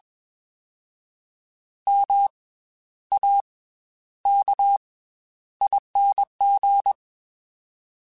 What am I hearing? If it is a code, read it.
Morse code, heard as a keyed tone, about 21 wpm, 800 Hz. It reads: M A K ING